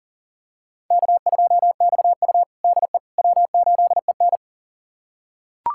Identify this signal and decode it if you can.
Morse code, heard as a keyed tone, about 40 wpm, 700 Hz. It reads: K2XU DE W8EN
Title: USA Callsign DE Callsign - 4 Characters - 40wpm